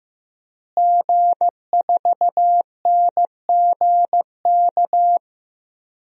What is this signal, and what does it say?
Morse code, heard as a keyed tone, about 15 wpm, 700 Hz. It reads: G4NGK
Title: DXCC-100 Call Signs - 5 Characters - Repeated 3x - 15wpm